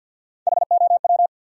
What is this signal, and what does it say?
Morse code, heard as a keyed tone, about 64 wpm, 700 Hz. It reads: HOW